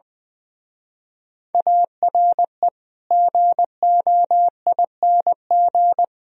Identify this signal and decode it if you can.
Morse code, heard as a keyed tone, about 20 wpm, 700 Hz. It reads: ARE GOING
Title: Speed-Racing - Sets of 4 Words 20wpm